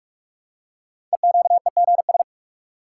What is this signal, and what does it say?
Morse code, heard as a keyed tone, about 44 wpm, 700 Hz. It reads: EQEGS
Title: Code Groups - 5 Letters - 45wpm